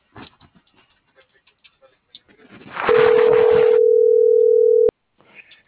Ringback tone (North America), 440 + 480 Hz.